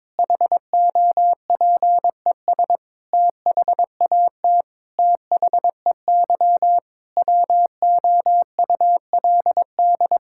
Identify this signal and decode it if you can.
Morse code, heard as a keyed tone, about 22 wpm, 700 Hz. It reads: HOPES THAT THEY WOULD
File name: Sentences with 9-Word Phrases - 22wpm